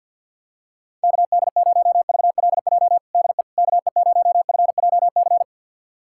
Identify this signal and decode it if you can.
Morse code, heard as a keyed tone, about 51 wpm, 700 Hz. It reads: KD0VLJ DE KE0FJC
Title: USA Callsign DE Callsign - 6 Characters - 50wpm